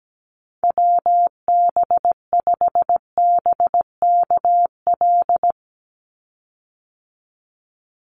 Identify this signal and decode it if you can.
Morse code, heard as a keyed tone, about 17 wpm, 700 Hz. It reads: WB5BKL
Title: Call Signs - 17wpm